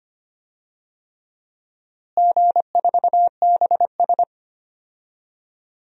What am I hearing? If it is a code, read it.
Morse code, heard as a keyed tone, about 25 wpm, 700 Hz. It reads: G4BS
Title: DXCC-100 Call Signs - 4 Characters - Repeated 3x - 25wpm